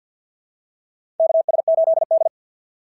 Morse code, heard as a keyed tone, about 50 wpm, 690 Hz. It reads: KS8D
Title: SOTA QSO Practice 50wpm